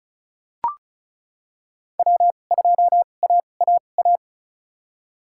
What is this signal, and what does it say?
Morse code, heard as a keyed tone, about 36 wpm, spaced as for 25 wpm, 700 Hz. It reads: W2AAA
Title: USA Call Signs - 5 Characters - 2x Character Spacing - 35wpm